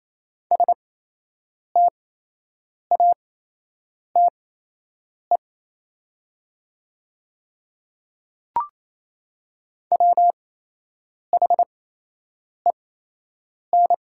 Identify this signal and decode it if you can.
Morse code, heard as a keyed tone, about 28 wpm, spaced as for 8 wpm, 700 Hz. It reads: STATE WHEN
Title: 8x Character Spacing - Top 500 Words - 28wpm